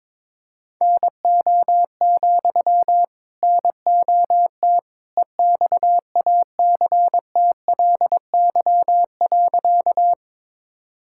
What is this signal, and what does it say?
Morse code, heard as a keyed tone, about 22 wpm, 700 Hz. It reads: NO, NOT EXACTLY.
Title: Sentences from Top 1000 Words - Review Part 6 - 22wpm